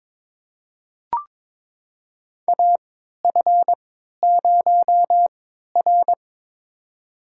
Morse code, heard as a keyed tone, about 22 wpm, spaced as for 13 wpm, 700 Hz. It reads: AF0R
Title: USA Call Signs - 3x Character Spacing - 22wpm